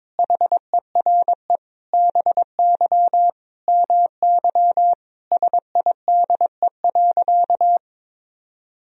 Morse code, heard as a keyed tone, about 21 wpm, 700 Hz. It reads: HERE BY MY SIDE.
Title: Sentences from Top 1000 Words - Review Part 1 - 22wpm